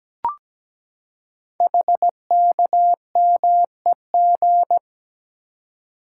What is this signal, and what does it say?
Morse code, heard as a keyed tone, about 17 wpm, 700 Hz. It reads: HKMEG
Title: Code Groups - 5 Letters - 17wpm